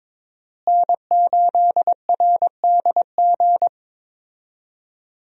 Morse code, heard as a keyed tone, about 22 wpm, 700 Hz. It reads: N8RDG
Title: USA Call Signs - 5 Characters - Repeated 2x - 22wpm